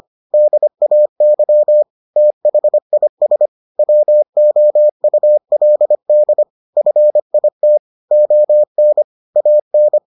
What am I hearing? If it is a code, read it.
Morse code, heard as a keyed tone, about 25 wpm, 600 Hz. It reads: DAY THIS WOULD FIT ON AN